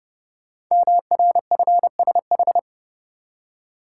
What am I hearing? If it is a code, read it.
Morse code, heard as a keyed tone, about 29 wpm, 700 Hz. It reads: MRFSH
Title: Code Groups - 5 Letters - 30wpm